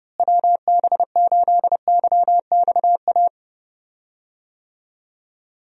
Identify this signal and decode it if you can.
Morse code, heard as a keyed tone, about 30 wpm, 700 Hz. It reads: WB8YXA
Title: USA Call Signs - 6 Characters - Repeated 2x - 30wpm